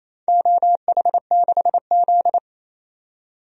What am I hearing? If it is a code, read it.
Morse code, heard as a keyed tone, about 28 wpm, 700 Hz. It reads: OH6Z
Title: DXCC-100 Call Signs - 4 Characters - 28wpm